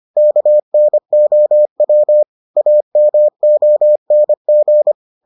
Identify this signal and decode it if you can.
Morse code, heard as a keyed tone, about 25 wpm, 600 Hz. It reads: KNOW AMONG